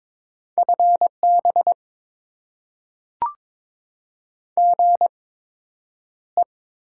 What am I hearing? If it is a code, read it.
Morse code, heard as a keyed tone, about 22 wpm, spaced as for 9 wpm, 700 Hz. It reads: FB GE
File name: Top 20 QSO Elements - 8x Character Spacing - 22wpm